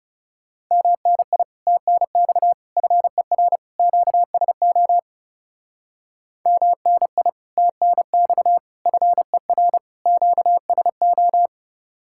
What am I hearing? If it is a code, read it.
Morse code, heard as a keyed tone, about 37 wpm, 700 Hz. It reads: MNI TNX FER QSO MNI TNX FER QSO
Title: Speed-Racing Common QSO Phrases - 30wpm